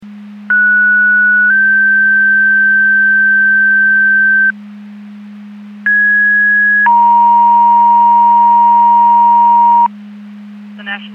Two-tone page, heard as a one-second tone, then a three-second tone, about 1470 then 1550 Hz, about 1650 then 940 Hz. It is heard from 0.5 to 9.9 s.
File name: Portage / 05840 19-36 Portage 154.385 Paging